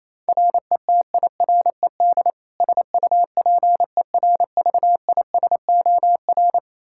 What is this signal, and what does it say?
Morse code, heard as a keyed tone, about 28 wpm, 700 Hz. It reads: RETIRED SUPERVISOR
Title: K1USN SST QSO Practice 28wpm